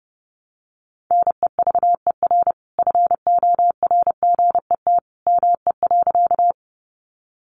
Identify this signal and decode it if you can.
Morse code, heard as a keyed tone, about 30 wpm, 700 Hz. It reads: NEVER FORGET ME.
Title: Sentences from Top 1200 Words 30wpm